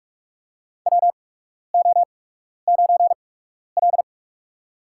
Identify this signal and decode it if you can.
Morse code, heard as a keyed tone, about 44 wpm, spaced as for 12 wpm, 700 Hz. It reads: WO9L
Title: USA Call Signs - 4 Characters - 8x Character Spacing - 45wpm